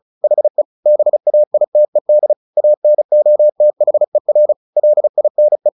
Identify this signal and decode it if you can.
Morse code, heard as a keyed tone, about 34 wpm, 600 Hz. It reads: HE BAITED ANOTHER LINE
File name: the old man and the sea / the old man and the sea 1 35wpm